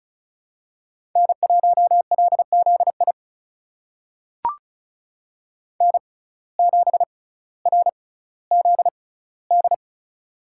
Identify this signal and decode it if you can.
Morse code, heard as a keyed tone, about 36 wpm, spaced as for 18 wpm, 700 Hz. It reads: N1LZI N7RZD
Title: USA Call Signs - 5 Characters - 6x Character Spacing - 35wpm